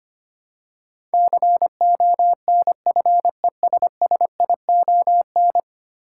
Morse code, heard as a keyed tone, about 25 wpm, 700 Hz. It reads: CONFESSION
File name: 10 Letter Words - First 3 Letters Given - 25wpm